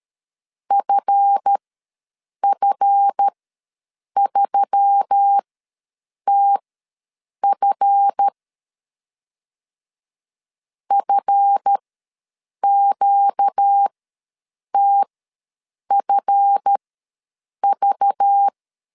Morse code, heard as a keyed tone, about 13 wpm, spaced as for 7 wpm, 800 Hz. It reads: FF3TF FQTFV
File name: LEZIONI / 7 GF9Z / 7 GF ws 12wpm 3140